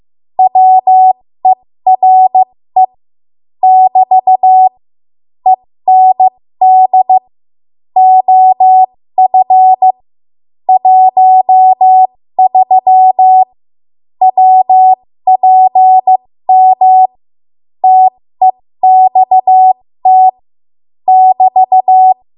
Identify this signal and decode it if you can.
Morse code, heard as a keyed tone, about 15 wpm, spaced as for 13 wpm, 750 Hz. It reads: WERE <BT> END OF 13 WPM TEXT <BT>
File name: Archive / 13 WPM / 13WPM